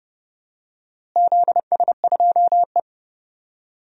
Morse code, heard as a keyed tone, about 30 wpm, 700 Hz. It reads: ZS2E
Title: DXCC-100 Call Signs - 4 Characters - Repeated 3x - 30wpm